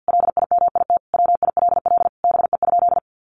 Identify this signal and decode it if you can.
Morse code, heard as a keyed tone, about 50 wpm, 700 Hz. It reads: LIMIT WILL BE?
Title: Sentences from Top 1500 Words 50wpm